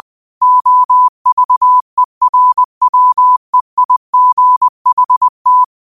Morse code, heard as a keyed tone, about 20 wpm, 1000 Hz. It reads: OVERWEIGHT